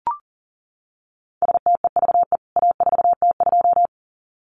Morse code, heard as a keyed tone, about 40 wpm, 700 Hz. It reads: STEVE A4T2
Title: CWT Exchange 40wpm